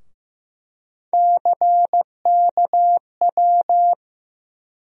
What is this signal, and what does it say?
Morse code, heard as a keyed tone, about 15 wpm, 700 Hz. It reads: CKW
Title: USA Call Sign Postfixes - 3 Letters 15wpm